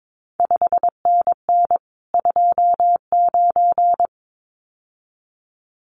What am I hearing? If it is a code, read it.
Morse code, heard as a keyed tone, about 22 wpm, 700 Hz. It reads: 5NN 29
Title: CQ WW DX Contest Exchange Encore - 22wpm